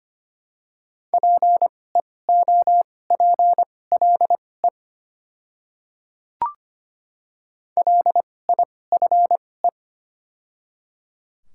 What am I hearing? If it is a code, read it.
Morse code, heard as a keyed tone, about 25 wpm, spaced as for 18 wpm, 700 Hz. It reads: PEOPLE LIFE
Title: 2x Character Spacing - Top 500 Words - 25wpm